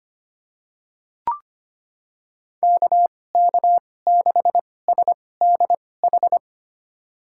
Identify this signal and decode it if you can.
Morse code, heard as a keyed tone, about 25 wpm, spaced as for 18 wpm, 700 Hz. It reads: KK6SDH
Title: USA Call Signs - 6 Characters - 2x Character Spacing - 25wpm